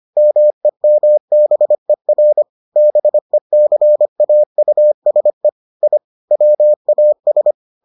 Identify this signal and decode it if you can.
Morse code, heard as a keyed tone, about 25 wpm, 600 Hz. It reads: MEMBER BECAUSE I WAS